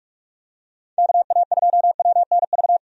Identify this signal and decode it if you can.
Morse code, heard as a keyed tone, about 46 wpm, 700 Hz. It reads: KA2WNV
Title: USA Call Signs - 6 Characters - 45wpm